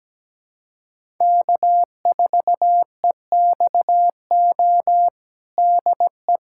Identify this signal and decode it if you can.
Morse code, heard as a keyed tone, about 17 wpm, 700 Hz. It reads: K4EXO DE
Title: USA Callsign DE Callsign 17wpm